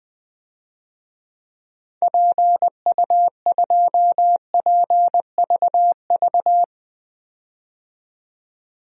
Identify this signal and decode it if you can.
Morse code, heard as a keyed tone, about 20 wpm, 700 Hz. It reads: PU2PVV
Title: DXCC-100 Call Signs - 6 Characters - 20wpm